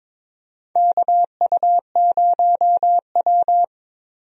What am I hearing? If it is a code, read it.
Morse code, heard as a keyed tone, about 21 wpm, 700 Hz. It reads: KU0W
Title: USA Call Signs - 4 Characters - 22wpm